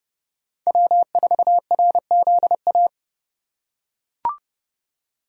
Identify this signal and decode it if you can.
Morse code, heard as a keyed tone, about 30 wpm, 700 Hz. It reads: W4RZA
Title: USA Call Signs - 5 Characters - 3x Character Spacing - 30wpm